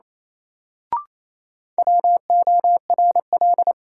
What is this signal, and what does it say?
Morse code, heard as a keyed tone, about 28 wpm, 700 Hz. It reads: WORL
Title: Sets of 2 Words - New Word Drill - 28wpm